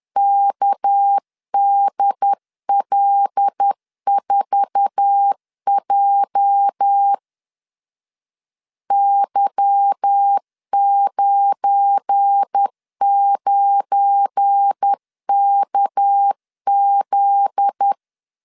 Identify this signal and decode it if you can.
Morse code, heard as a keyed tone, about 10 wpm, 800 Hz. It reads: KDL4J Y99KZ
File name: CW / DET / 1 9 1500C S B 10wpm 4439